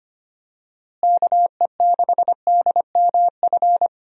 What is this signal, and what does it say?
Morse code, heard as a keyed tone, about 25 wpm, 700 Hz. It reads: KE6DMF
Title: USA Call Signs - 6 Characters - Repeated 3x - 25wpm